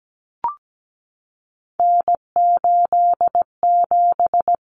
Morse code, heard as a keyed tone, about 17 wpm, 700 Hz. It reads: N87
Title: Common Cut Numbers - 3-Digit 17wpm